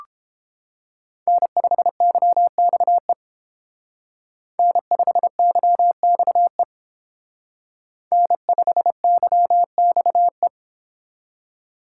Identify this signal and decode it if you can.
Morse code, heard as a keyed tone, about 29 wpm, 700 Hz. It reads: N5YXE N5YXE N5YXE